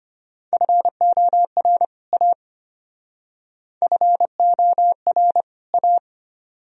Morse code, heard as a keyed tone, about 31 wpm, 700 Hz. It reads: FOR A FOR A
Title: Speed-Racing - Sets of 2 Words 25wpm